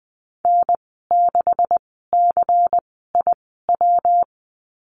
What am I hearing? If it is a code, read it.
Morse code, heard as a keyed tone, about 19 wpm, spaced as for 14 wpm, 700 Hz. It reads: N6CIW
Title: 161 Lesson 26g - Letter g - Call Signs 20wpm